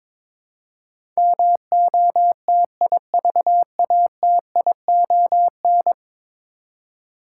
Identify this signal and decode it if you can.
Morse code, heard as a keyed tone, about 22 wpm, 700 Hz. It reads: MOTIVATION
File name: practice / 22wpm / 10 Letter Words - First 3 Letters Given - 22wpm